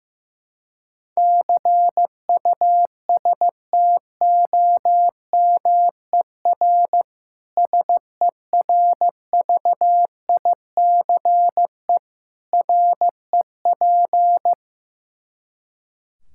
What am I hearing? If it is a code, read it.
Morse code, heard as a keyed tone, about 16 wpm, 700 Hz. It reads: CUSTOMER SERVICE REP
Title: Practice QSO - Exchange 3 - More Info - 15wpm